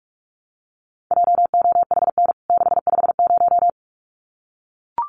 Morse code, heard as a keyed tone, about 44 wpm, 700 Hz. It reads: JOHN 650
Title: CWT Exchange 45wpm